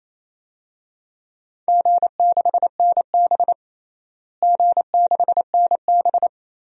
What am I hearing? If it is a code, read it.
Morse code, heard as a keyed tone, about 28 wpm, 700 Hz. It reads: G6NB G6NB
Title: DXCC-100 Call Signs - 4 Characters - Repeated 2x - 28wpm